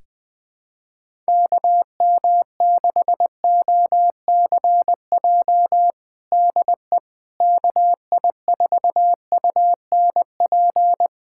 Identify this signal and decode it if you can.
Morse code, heard as a keyed tone, about 20 wpm, 700 Hz. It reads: KM6OCJ DE KI4UNP
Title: USA Callsign DE Callsign - 6 Characters - 20wpm